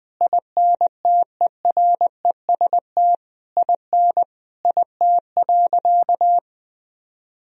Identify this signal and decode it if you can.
Morse code, heard as a keyed tone, about 21 wpm, 700 Hz. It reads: INTEREST IN IT.
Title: Sentences from Top 1000 Words - Review Part 2 - 20wpm